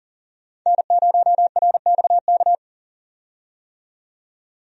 Morse code, heard as a keyed tone, about 40 wpm, 700 Hz. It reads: N0RXK
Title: USA Call Signs - 5 Characters - 40wpm